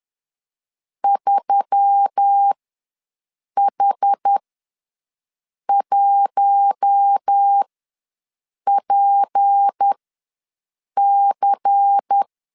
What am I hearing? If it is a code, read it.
Morse code, heard as a keyed tone, about 10 wpm, spaced as for 6 wpm, 800 Hz. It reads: 3H1PC
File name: LEZIONI / 7 GF9Z / 7 G ws 10wpm 3843